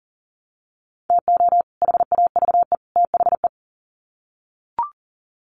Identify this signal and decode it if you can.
Morse code, heard as a keyed tone, about 40 wpm, 700 Hz. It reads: TO HAVE THE
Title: Sets of 3 Words 40wpm